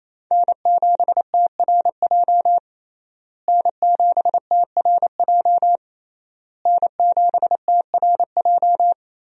Morse code, heard as a keyed tone, about 28 wpm, 700 Hz. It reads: N7TRJ N7TRJ N7TRJ
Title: USA Call Signs - 5 Characters - Repeated 3x - 28wpm